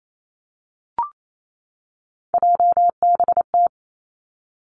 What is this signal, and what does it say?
Morse code, heard as a keyed tone, about 28 wpm, 690 Hz. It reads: JBT
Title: Three Challenging Letter-Number Combinations 28wpm